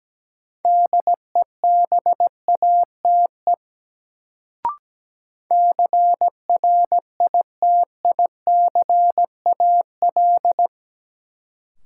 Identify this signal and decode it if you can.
Morse code, heard as a keyed tone, about 17 wpm, 700 Hz. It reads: DEBATE CRITICAL
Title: Sentences from Top 1100 Words - New Word Drill - 17wpm